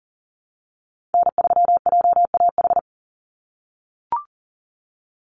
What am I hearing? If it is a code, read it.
Morse code, heard as a keyed tone, about 40 wpm, 700 Hz. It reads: N3JAH
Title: USA Call Signs 40wpm